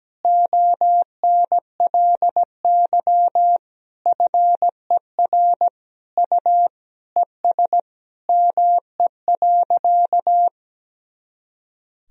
Morse code, heard as a keyed tone, about 17 wpm, 700 Hz. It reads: ONLY FER U ES ME.
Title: Sentences from Top 500 Words and Abbreviations 17wpm